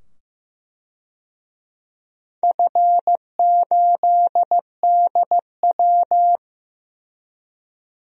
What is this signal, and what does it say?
Morse code, heard as a keyed tone, about 15 wpm, 700 Hz. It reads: F8DW
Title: DXCC-100 Call Signs - 4 Characters - 15wpm